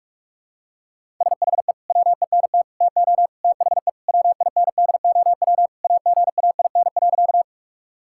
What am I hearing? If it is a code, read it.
Morse code, heard as a keyed tone, about 46 wpm, 700 Hz. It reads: SHE WENT TO THE WINDOW AGAIN.
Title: Sentences from Top 1600 Words 45wpm